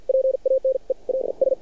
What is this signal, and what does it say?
Morse code, heard as a keyed tone, about 49 wpm, spaced as for 38 wpm, 530 Hz. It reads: PANE AS
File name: archived / CRSnet